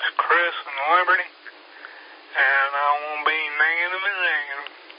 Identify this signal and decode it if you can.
Emergency Alert System (SAME) tones, heard as harsh buzzing bursts, with speech. It is heard from 1.1 to 3.9 s.
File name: SimplexLogic